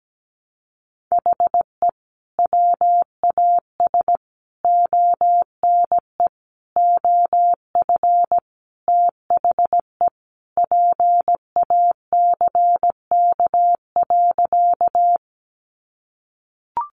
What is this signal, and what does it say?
Morse code, heard as a keyed tone, about 17 wpm, 700 Hz. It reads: HE WAS ONE OF THE PACK.
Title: Sentences from Top 1300 Words 17wpm